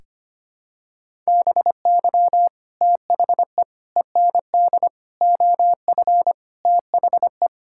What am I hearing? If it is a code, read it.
Morse code, heard as a keyed tone, about 25 wpm, 700 Hz. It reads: BY THE END OF THE
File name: practice / 25wpm / Sentences with 4-Word Phrases - 25wpm